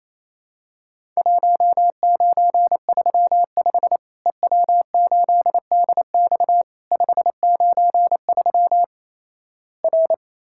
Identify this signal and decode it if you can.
Morse code, heard as a keyed tone, about 28 wpm, 700 Hz. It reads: 1935 EW8DX 593 R
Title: Worked All Europe DX Contest - QTC Exchange - 28wpm